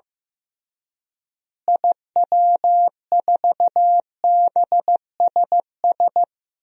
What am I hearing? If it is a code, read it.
Morse code, heard as a keyed tone, about 16 wpm, 700 Hz. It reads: IW4BSS
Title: DXCC-100 Call Signs - 6 Characters - Repeated 3x - 15wpm